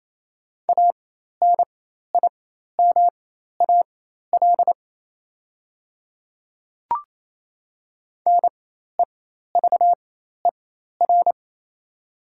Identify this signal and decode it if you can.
Morse code, heard as a keyed tone, about 28 wpm, spaced as for 13 wpm, 700 Hz. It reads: ANIMAL NEVER
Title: 4x Character Spacing - Top 300 Words - 28wpm